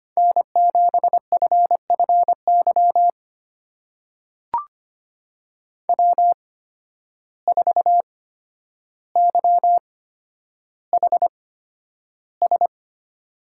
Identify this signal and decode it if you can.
Morse code, heard as a keyed tone, about 25 wpm, spaced as for 11 wpm, 700 Hz. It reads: N7FFY W4YHS
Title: USA Call Signs - 5 Characters - 8x Character Spacing - 25wpm